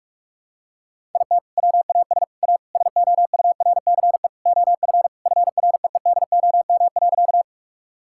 Morse code, heard as a keyed tone, about 46 wpm, 700 Hz. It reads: IT WAS A SOURCE OF FREEDOM.
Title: Sentences from Top 1000 Words - Review Part 5 - 45wpm